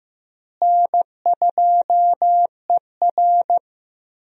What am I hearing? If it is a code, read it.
Morse code, heard as a keyed tone, about 15 wpm, 700 Hz. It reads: N2ER
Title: USA Call Signs - 4 Characters - Repeated 3x - 15wpm